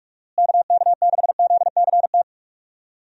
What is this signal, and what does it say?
Morse code, heard as a keyed tone, about 46 wpm, 700 Hz. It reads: KK6ZCT
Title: USA Call Signs - 6 Characters - Repeated 3x - 45wpm